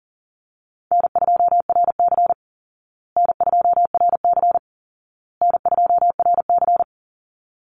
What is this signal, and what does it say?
Morse code, heard as a keyed tone, about 37 wpm, 700 Hz. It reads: N2RC N2RC N2RC
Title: Easy Call Signs Repeated 3x - 40wpm